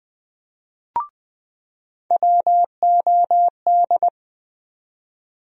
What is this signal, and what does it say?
Morse code, heard as a keyed tone, about 20 wpm, 700 Hz. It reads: WOD
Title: USA Call Sign Postfixes - 3 Letters 20wpm